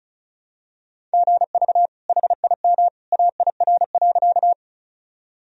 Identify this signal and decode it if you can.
Morse code, heard as a keyed tone, about 36 wpm, 700 Hz. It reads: GV HIM AIR.
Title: Sentences from Top 200 Words and Abbreviations 35wpm